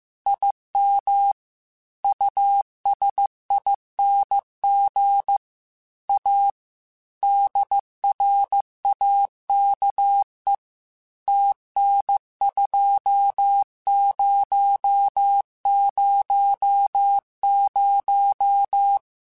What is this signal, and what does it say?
Morse code, heard as a keyed tone, about 15 wpm, 800 Hz. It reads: IM USING A DRAKE TN2000
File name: QSOs / QSOs-mp3s / WD0FBT 1@13